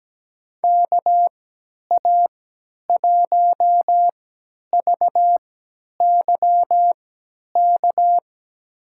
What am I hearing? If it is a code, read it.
Morse code, heard as a keyed tone, about 17 wpm, spaced as for 10 wpm, 700 Hz. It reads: KA1VYK
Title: USA Call Signs - 6 Characters - 3x Character Spacing - 17wpm